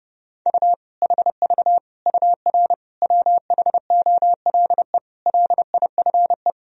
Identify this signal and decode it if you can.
Morse code, heard as a keyed tone, about 30 wpm, 700 Hz. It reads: U HV UR WHOLE LIFE
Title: Sentences from Top 400 Words and Abbreviations 30wpm